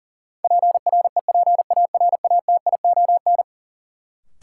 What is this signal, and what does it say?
Morse code, heard as a keyed tone, about 40 wpm, 700 Hz. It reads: PREPARATION
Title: Sentences from Top 1400 Words - New Word Drill - 40wpm